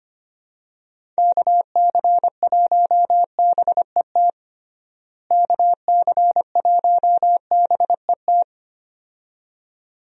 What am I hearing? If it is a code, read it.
Morse code, heard as a keyed tone, about 25 wpm, 700 Hz. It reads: KC1BET KC1BET
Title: USA Call Signs - 6 Characters - Repeated 2x - 25wpm